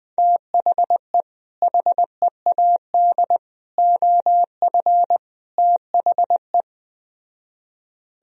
Morse code, heard as a keyed tone, about 20 wpm, 700 Hz. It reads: THE HEAD OF THE
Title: Speed-Racing - Sets of 4 Words 17wpm